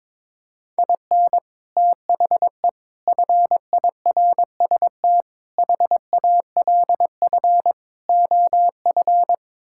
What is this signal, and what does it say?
Morse code, heard as a keyed tone, about 22 wpm, 700 Hz. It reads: IN THE FIRST HALF OF
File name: practice / 22wpm / Build A Phrase - 5-Word Phrases - 22wpm